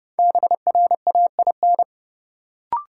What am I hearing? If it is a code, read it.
Morse code, heard as a keyed tone, about 30 wpm, 700 Hz. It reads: BRAIN
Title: Sentences from Top 1100 Words - New Word Drill Alt - 30wpm